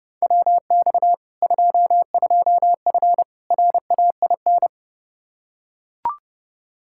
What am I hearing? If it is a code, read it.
Morse code, heard as a keyed tone, about 30 wpm, 700 Hz. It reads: WX 22F RAIN
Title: Practice QSO - Exchange 2 - Info - 30wpm